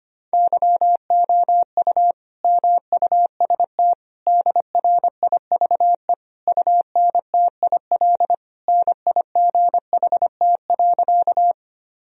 Morse code, heard as a keyed tone, about 25 wpm, 700 Hz. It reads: YOU MUST DRIVE UNTIL NIGHT.
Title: Sentences from Top 1000 Words - Review Part 2 - 25wpm